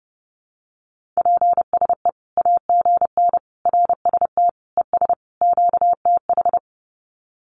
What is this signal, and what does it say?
Morse code, heard as a keyed tone, about 30 wpm, 700 Hz. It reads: PSE AGN RST ES QTH
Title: Common QSO Phrases 30wpm